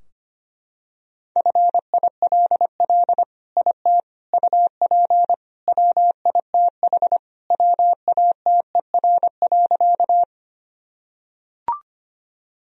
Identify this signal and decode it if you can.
Morse code, heard as a keyed tone, about 25 wpm, 700 Hz. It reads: FILL IT UP WITH WATER.
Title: Sentences from Top 1000 Words - Review Part 3 - 25wpm